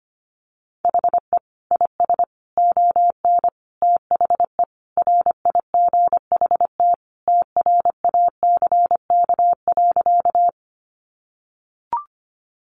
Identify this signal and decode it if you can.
Morse code, heard as a keyed tone, about 25 wpm, 690 Hz. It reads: HE IS ON THE RIGHT TRACK.
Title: Sentences from Top 1100 Words 25wpm